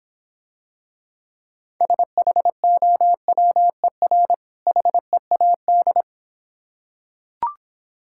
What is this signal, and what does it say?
Morse code, heard as a keyed tone, about 26 wpm, 700 Hz. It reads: SHOWER HEAD